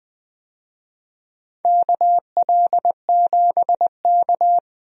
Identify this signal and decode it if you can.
Morse code, heard as a keyed tone, about 20 wpm, 700 Hz. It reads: KL7K
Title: DXCC-100 Call Signs - 4 Characters - 20wpm